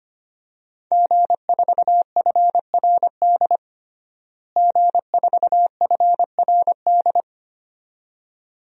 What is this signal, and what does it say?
Morse code, heard as a keyed tone, about 25 wpm, 700 Hz. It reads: G4FRD G4FRD
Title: DXCC-100 Call Signs - 5 Characters - Repeated 2x - 25wpm